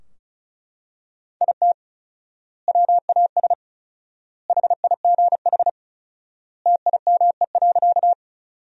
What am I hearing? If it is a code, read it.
Morse code, heard as a keyed tone, about 34 wpm, 700 Hz. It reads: IT WAS HIGH TIME.
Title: 4x Word-Spacing - Sentences from Top 200 Words - 35wpm